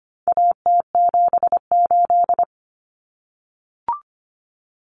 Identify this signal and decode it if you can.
Morse code, heard as a keyed tone, about 25 wpm, 700 Hz. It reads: AT78
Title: Common Cut Numbers - 4-Digit 25wpm